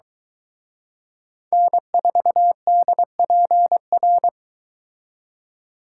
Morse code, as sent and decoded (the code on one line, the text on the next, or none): -. ....- -.. .--. .-.
N4DPR